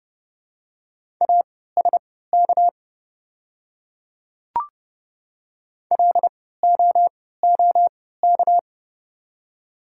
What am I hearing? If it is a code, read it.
Morse code, heard as a keyed tone, about 30 wpm, spaced as for 17 wpm, 700 Hz. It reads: ASK LOOK
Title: 3x Character Spacing - Top 300 Words - 30wpm